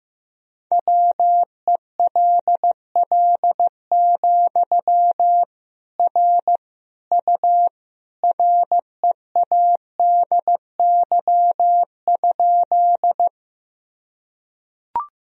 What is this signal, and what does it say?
Morse code, heard as a keyed tone, about 15 wpm, 700 Hz. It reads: WELL, R U READY?
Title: Sentences from Top 400 Words and Abbreviations 15wpm